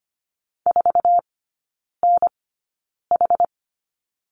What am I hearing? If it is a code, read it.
Morse code, heard as a keyed tone, about 25 wpm, spaced as for 9 wpm, 700 Hz. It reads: 4NH
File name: course / 25:10wpm / 093 Lesson 16e - Letter c - Three Characters 25wpm